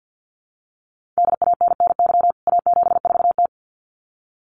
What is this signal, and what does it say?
Morse code, heard as a keyed tone, about 50 wpm, 700 Hz. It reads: DANNY A74T
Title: CWT Exchange 50wpm